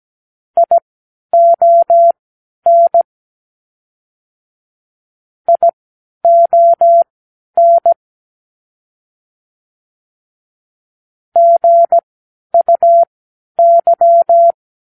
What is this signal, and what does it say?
Morse code, heard as a keyed tone, about 17 wpm, spaced as for 10 wpm, 680 Hz. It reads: ION ION GUY
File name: k7qo / Practicewords